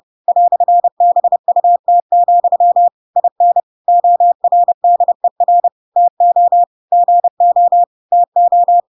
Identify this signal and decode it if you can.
Morse code, heard as a keyed tone, about 30 wpm, 700 Hz. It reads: "BUT, IN ORDER TO GO TO
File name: Pinocchio / Pinocchio ch8 30 30 ik1hge